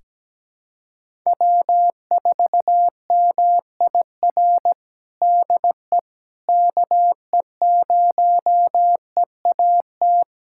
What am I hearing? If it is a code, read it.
Morse code, heard as a keyed tone, about 17 wpm, 700 Hz. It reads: W4MIR DE KE0EAT
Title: USA Callsign DE Callsign 17wpm